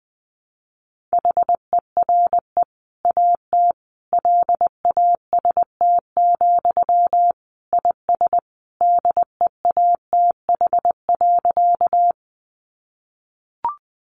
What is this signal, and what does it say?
Morse code, heard as a keyed tone, about 20 wpm, 700 Hz. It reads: HERE AT LAST, IS DEATH.
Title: Sentences from Top 1100 Words 20wpm